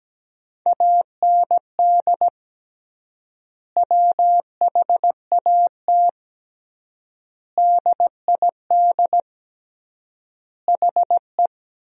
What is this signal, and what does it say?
Morse code, heard as a keyed tone, about 17 wpm, 700 Hz. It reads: AND WHAT DID HE
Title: 3x Word-Spacing - Sentences from Top 200 Words - 17wpm